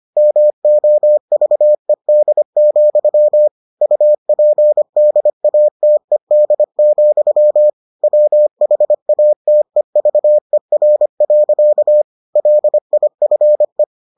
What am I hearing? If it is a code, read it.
Morse code, heard as a keyed tone, about 25 wpm, 600 Hz. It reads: MOVED, UPDATED, WHATEVER. LIFE